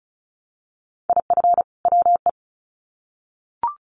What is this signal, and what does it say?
Morse code, heard as a keyed tone, about 34 wpm, 700 Hz. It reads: IF WE
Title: Sets of 2 Words 35wpm